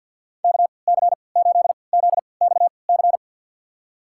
Morse code, heard as a keyed tone, about 52 wpm, spaced as for 28 wpm, 700 Hz. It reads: KC8ZXC
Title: USA Call Signs - 6 Characters - 3x Character Spacing - 50wpm